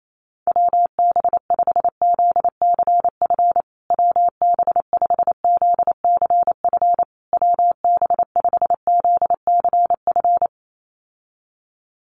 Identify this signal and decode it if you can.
Morse code, heard as a keyed tone, about 28 wpm, 700 Hz. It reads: WB5ZCF WB5ZCF WB5ZCF
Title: USA Call Signs Repeated 3x 28wpm